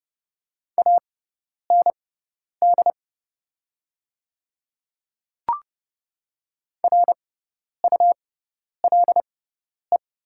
Morse code, heard as a keyed tone, about 30 wpm, spaced as for 10 wpm, 700 Hz. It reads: AND RULE
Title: 6x Character Spacing - Top 500 Words - 30wpm